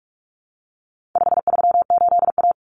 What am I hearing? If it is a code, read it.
Morse code, heard as a keyed tone, about 44 wpm, 700 Hz. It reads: 538A
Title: Common Cut Numbers - 4-Digit 45wpm